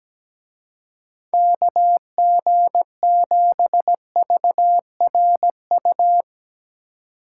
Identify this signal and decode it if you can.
Morse code, heard as a keyed tone, about 18 wpm, 700 Hz. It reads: KG7VRU
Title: Speed Racing - USA 2x3 Call Signs 17wpm